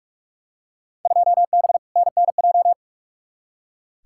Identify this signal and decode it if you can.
Morse code, heard as a keyed tone, about 46 wpm, 700 Hz. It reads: 2B NNJ
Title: Field Day Exchanges Encore - 45wpm